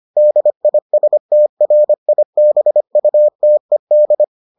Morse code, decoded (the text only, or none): DISTRIBUTED